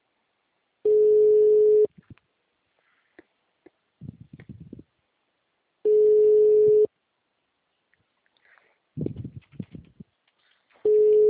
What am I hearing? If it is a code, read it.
Ringback tone (Europe, 425 Hz).